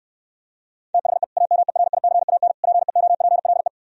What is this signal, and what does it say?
Morse code, heard as a keyed tone, about 66 wpm, 700 Hz. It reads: THE AMERICAN PEOPLE